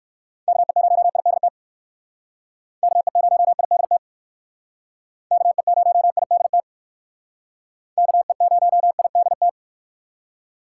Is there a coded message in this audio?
Yes: Morse code from 0.5 to 9.5 s, heard as a keyed tone, about 56 wpm, 700 Hz. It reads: KE0IDT KE0IDT KE0IDT KE0IDT